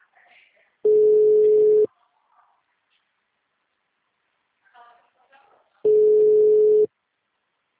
Ringback tone (Europe, 425 Hz).
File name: out- -2019- - - .26